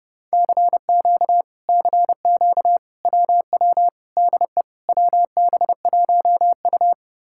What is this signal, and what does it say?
Morse code, heard as a keyed tone, about 33 wpm, 700 Hz. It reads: CQ CQ WW DE WB1U